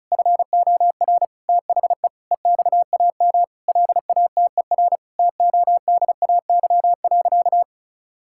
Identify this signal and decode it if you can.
Morse code, heard as a keyed tone, about 36 wpm, 700 Hz. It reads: FOR THE EXAM LATER TODAY.
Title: Sentences from Top 1000 Words - Review Part 5 - 35wpm